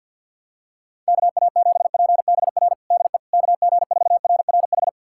Morse code, heard as a keyed tone, about 52 wpm, 700 Hz. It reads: KA7PBR DE KG4RRH
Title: USA Callsign DE Callsign - 6 Characters - 50wpm